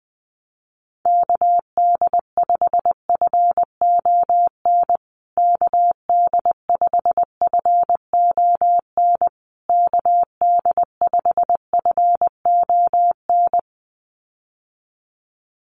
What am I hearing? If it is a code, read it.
Morse code, heard as a keyed tone, about 20 wpm, 700 Hz. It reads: KD5FON KD5FON KD5FON
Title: USA Call Signs Repeated 3x 20wpm